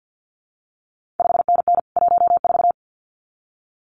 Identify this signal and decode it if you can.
Morse code, heard as a keyed tone, about 50 wpm, 700 Hz. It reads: 5NN 14
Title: CQ WW DX Contest Exchange - 50wpm